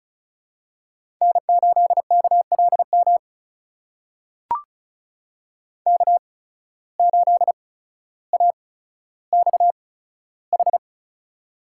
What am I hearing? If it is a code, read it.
Morse code, heard as a keyed tone, about 34 wpm, spaced as for 15 wpm, 700 Hz. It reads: N8KLM K8AXH
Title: USA Call Signs - 5 Characters - 8x Character Spacing - 35wpm